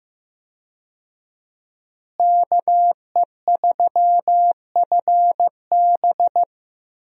Morse code, heard as a keyed tone, about 15 wpm, 700 Hz. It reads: KE3FB
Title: USA Call Signs - 5 Characters - Repeated 3x - 15wpm